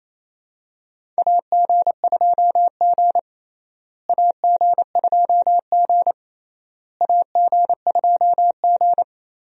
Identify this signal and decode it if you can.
Morse code, heard as a keyed tone, about 28 wpm, 700 Hz. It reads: AG2G AG2G AG2G
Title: USA Call Signs - 4 Characters - Repeated 3x - 28wpm